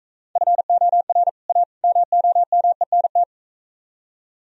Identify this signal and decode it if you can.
Morse code, heard as a keyed tone, about 44 wpm, 700 Hz. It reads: FOR A MOMENT